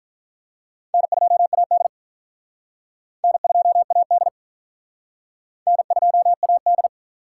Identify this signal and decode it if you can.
Morse code, heard as a keyed tone, about 47 wpm, 700 Hz. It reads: N2AD N2AD N2AD